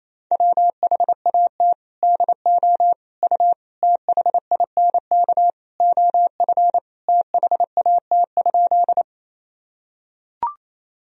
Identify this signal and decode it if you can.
Morse code, heard as a keyed tone, about 28 wpm, 700 Hz. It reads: WHAT DO U THINK OF THAT?
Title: Sentences from Top 200 Words and Abbreviations 28wpm